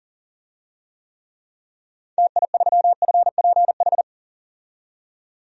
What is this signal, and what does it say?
Morse code, heard as a keyed tone, about 40 wpm, 700 Hz. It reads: TI3FPH
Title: DXCC-100 Call Signs - 6 Characters - 40wpm